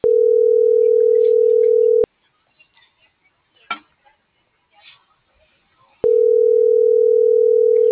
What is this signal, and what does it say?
Ringback tone (North America), 440 + 480 Hz.